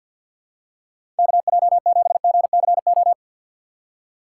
Morse code, heard as a keyed tone, about 47 wpm, 690 Hz. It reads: KJ7GCO
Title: USA Call Signs - 6 Characters - 8x Character Spacing - 50wpm